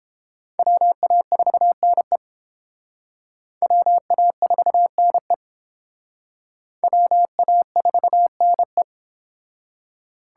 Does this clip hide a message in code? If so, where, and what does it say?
Yes: Morse code from 0.6 to 8.8 s, heard as a keyed tone, about 28 wpm, 700 Hz. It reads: WA4NE WA4NE WA4NE